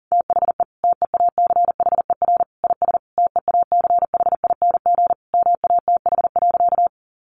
Morse code, heard as a keyed tone, about 40 wpm, 700 Hz. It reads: THE TEACHER IS TEACHING MATH.
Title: Easy Phrases 40wpm